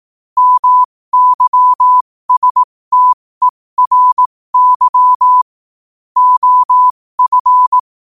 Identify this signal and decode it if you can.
Morse code, heard as a keyed tone, about 18 wpm, spaced as for 15 wpm, 1000 Hz. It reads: MYSTERY OF
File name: qotdfiles / 15wpm